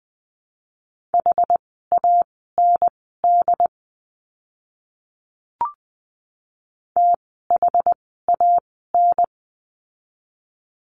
Morse code, heard as a keyed tone, about 20 wpm, spaced as for 14 wpm, 700 Hz. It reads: HAND THAN
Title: 115 Lesson 19f - Letter w - Words 20wpm